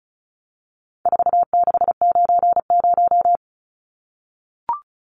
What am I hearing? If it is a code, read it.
Morse code, heard as a keyed tone, about 34 wpm, 700 Hz. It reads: 4690
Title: Numbers - 4-Digit 35wpm